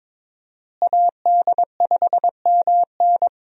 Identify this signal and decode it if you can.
Morse code, heard as a keyed tone, about 22 wpm, 700 Hz. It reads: AD5MN
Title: USA Call Signs - 5 Characters - Repeated 3x - 22wpm